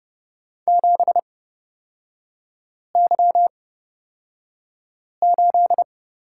Morse code, heard as a keyed tone, about 31 wpm, spaced as for 5 wpm, 690 Hz. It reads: 7Y8
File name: Lesson 38e - Letter q - Three Characters 30wpm